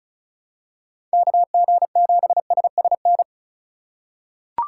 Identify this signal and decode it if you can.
Morse code, heard as a keyed tone, about 36 wpm, 700 Hz. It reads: KG7SSN
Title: USA Call Signs - 6 Characters - 3x Character Spacing - 35wpm